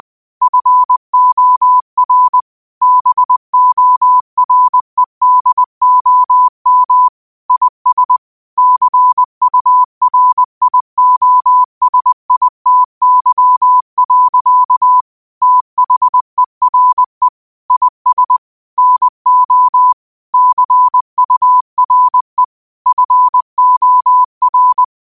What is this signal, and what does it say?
Morse code, heard as a keyed tone, about 20 wpm, 1000 Hz. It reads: FOR BOREDOM IS CURIOSITY. THERE IS NO CURE FOR